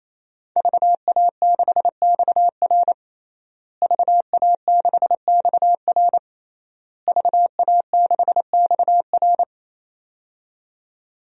Morse code, heard as a keyed tone, about 27 wpm, 700 Hz. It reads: VA6XR VA6XR VA6XR
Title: DXCC-100 Call Signs - 5 Characters - Repeated 3x - 28wpm